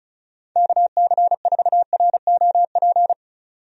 Morse code, heard as a keyed tone, about 34 wpm, 700 Hz. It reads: KC4ROP
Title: USA Call Signs - 6 Characters - 35wpm